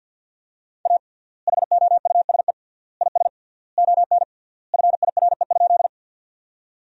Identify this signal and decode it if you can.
Morse code, heard as a keyed tone, about 46 wpm, 700 Hz. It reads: A HOUSE IS ON FIRE?
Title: 3x Word-Spacing - Sentences from Top 400 Words - 50wpm